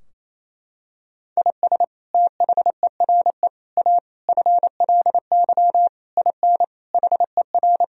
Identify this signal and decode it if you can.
Morse code, heard as a keyed tone, about 28 wpm, 700 Hz. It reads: IS THERE A FLY IN HER
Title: Sentences from Top 1000 Words - Review Part 4 - 28wpm